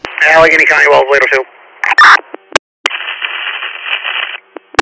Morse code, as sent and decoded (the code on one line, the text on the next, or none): none